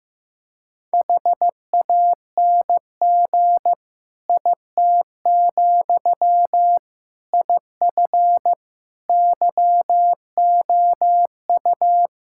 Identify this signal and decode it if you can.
Morse code, heard as a keyed tone, about 15 wpm, 700 Hz. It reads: HANG IT, IF YOU
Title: Sentences from Top 1600 Words 15wpm